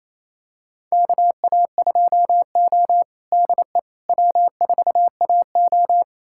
Morse code, heard as a keyed tone, about 28 wpm, 690 Hz. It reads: KA2O DE W4AO
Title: USA Callsign DE Callsign - 4 Characters - 28wpm